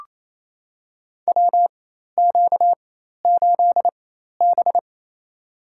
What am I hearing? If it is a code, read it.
Morse code, heard as a keyed tone, about 28 wpm, spaced as for 13 wpm, 700 Hz. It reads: WQ8B
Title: USA Call Signs - 4x Character Spacing - 28wpm